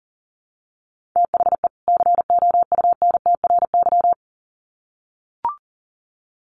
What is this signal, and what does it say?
Morse code, heard as a keyed tone, about 40 wpm, 700 Hz. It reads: THE COUNTRY
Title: Sets of 2 Words 40wpm